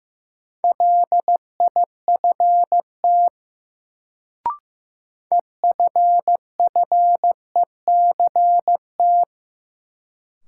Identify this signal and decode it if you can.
Morse code, heard as a keyed tone, about 15 wpm, 700 Hz. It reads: LIFT EFFECT
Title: Sentences from Top 1000 Words - New Word Drill - 15wpm